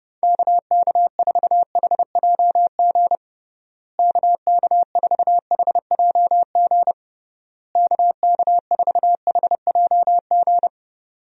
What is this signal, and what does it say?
Morse code, heard as a keyed tone, about 32 wpm, 700 Hz. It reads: KK4HJG KK4HJG KK4HJG
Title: USA Call Signs - 6 Characters - Repeated 3x - 30wpm